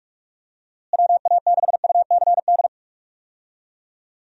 Morse code, heard as a keyed tone, about 46 wpm, 700 Hz. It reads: WA6UCD
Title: USA Call Signs - 6 Characters - Repeated 3x - 45wpm